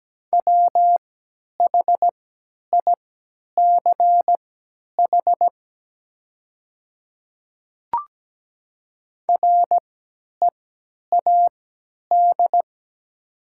Morse code, heard as a keyed tone, about 17 wpm, spaced as for 10 wpm, 700 Hz. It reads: WHICH READ
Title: 3x Character Spacing - Top 300 Words - 17wpm